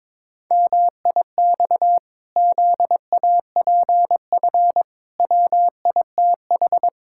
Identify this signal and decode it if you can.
Morse code, heard as a keyed tone, about 22 wpm, 700 Hz. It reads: MIX ZAPF WITH
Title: Pangram Challenge v2 - 22wpm